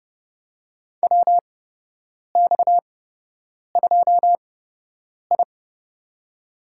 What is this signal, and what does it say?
Morse code, heard as a keyed tone, about 30 wpm, spaced as for 8 wpm, 700 Hz. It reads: WX2I
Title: USA Call Signs - 4 Characters - 8x Character Spacing - 30wpm